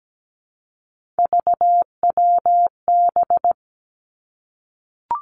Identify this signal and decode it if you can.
Morse code, heard as a keyed tone, about 17 wpm, 700 Hz. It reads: VWB
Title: Three Challenging Letter-Number Combinations 17wpm